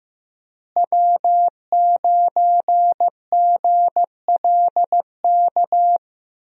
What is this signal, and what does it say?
Morse code, heard as a keyed tone, about 15 wpm, 700 Hz. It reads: W9GLK
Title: USA Call Signs - 5 Characters - Repeated 2x - 15wpm